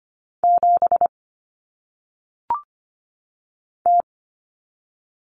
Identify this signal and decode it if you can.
Morse code, heard as a keyed tone, about 25 wpm, 700 Hz. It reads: 7 T
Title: Common Cut Numbers - 1-Digit 25wpm